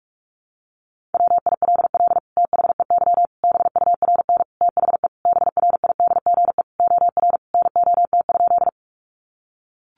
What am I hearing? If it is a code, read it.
Morse code, heard as a keyed tone, about 44 wpm, 700 Hz. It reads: WILL THEY BURN THE BRIDGE OR NOT?
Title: Sentences from Top 1300 Words 45wpm